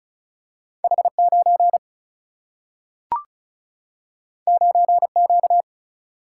Morse code, heard as a keyed tone, about 36 wpm, 700 Hz. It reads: H9 9Q
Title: Single-Letter Single-Number 35wpm